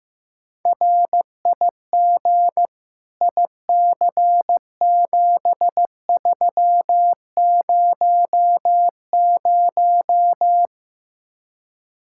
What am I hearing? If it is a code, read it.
Morse code, heard as a keyed tone, about 15 wpm, 700 Hz. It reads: RIG IC7300
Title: Speed-Racing Common QSO Phrases Encore - 15wpm